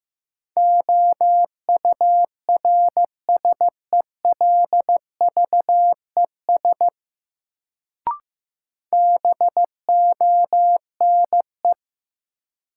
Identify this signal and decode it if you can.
Morse code, heard as a keyed tone, about 15 wpm, 700 Hz. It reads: OURSELVES BONE
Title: Sentences from Top 1600 Words - New Word Drill - 15wpm